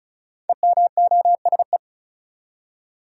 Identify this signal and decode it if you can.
Morse code, heard as a keyed tone, about 36 wpm, 700 Hz. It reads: EMOSE
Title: Code Groups - 5 Letters - 35wpm